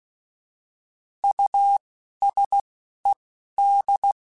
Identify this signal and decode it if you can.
Morse code, heard as a keyed tone, about 16 wpm, spaced as for 11 wpm, 800 Hz. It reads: USED
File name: Alphabetical / Alphabetical mp3s / Us @10